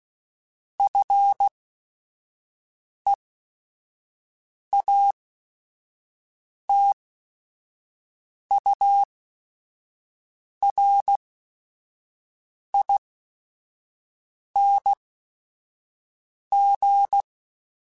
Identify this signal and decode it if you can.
Morse code, heard as a keyed tone, about 16 wpm, spaced as for 5 wpm, 800 Hz. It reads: FEATURING
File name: Special INGs / Special INGs-mp3s / Sngl Ing @5